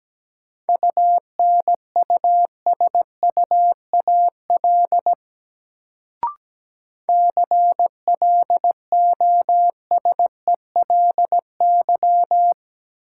Morse code, heard as a keyed tone, about 17 wpm, 700 Hz. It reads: UNUSUAL CLOSELY
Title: Sentences from Top 1400 Words - New Word Drill Alt - 17wpm